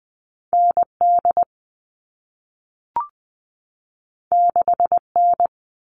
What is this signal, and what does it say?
Morse code, heard as a keyed tone, about 20 wpm, 700 Hz. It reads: ND 6N
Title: All Cut Numbers - 2-Digit 20wpm